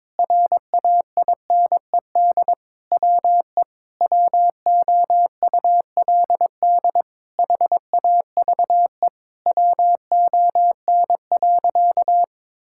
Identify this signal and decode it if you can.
Morse code, heard as a keyed tone, about 22 wpm, 700 Hz. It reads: RAINED WE WOULD HAVE WON.
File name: Sentences with 7-Word Phrases - 22wpm